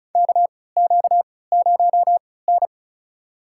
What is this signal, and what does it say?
Morse code, heard as a keyed tone, about 34 wpm, spaced as for 20 wpm, 700 Hz. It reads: KQ0N
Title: USA Call Signs - 4 Characters - 3x Character Spacing - 35wpm